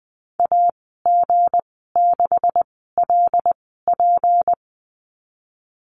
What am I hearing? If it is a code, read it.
Morse code, heard as a keyed tone, about 20 wpm, spaced as for 15 wpm, 700 Hz. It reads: AG6LP
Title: 179 Lesson 29g - Slash - Call Signs 20wpm